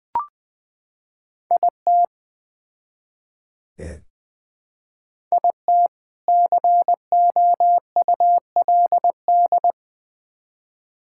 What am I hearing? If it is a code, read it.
Morse code, heard as a keyed tone, about 20 wpm, 700 Hz. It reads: IT IT COULD
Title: Build A Phrase - 3-Word Phrases - 20wpm